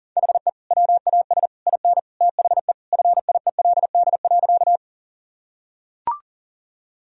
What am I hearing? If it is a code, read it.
Morse code, heard as a keyed tone, about 40 wpm, 700 Hz. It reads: HE WAS IN THE FIELD.
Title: Sentences from Top 1000 Words - Review Part 2 - 40wpm